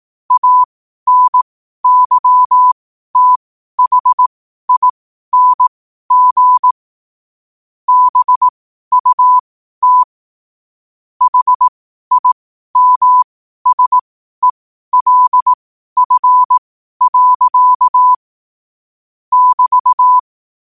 Morse code, heard as a keyed tone, about 18 wpm, spaced as for 12 wpm, 1000 Hz. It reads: ANYTHING BUT HIMSELF. <BT>
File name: qotdfiles / 12wpm